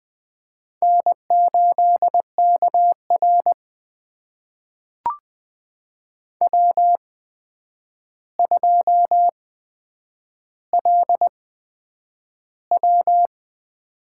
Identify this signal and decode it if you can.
Morse code, heard as a keyed tone, about 20 wpm, spaced as for 9 wpm, 700 Hz. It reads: N8KR W2LW
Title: USA Call Signs - 4 Characters - 8x Character Spacing - 20wpm